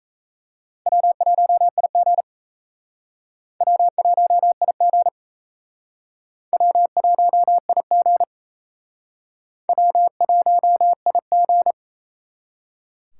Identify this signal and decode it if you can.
Morse code, heard as a keyed tone, about 35 wpm, 700 Hz. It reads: W1IG W1IG W1IG W1IG
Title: Speed Racing - USA 1x2 Call Signs 28wpm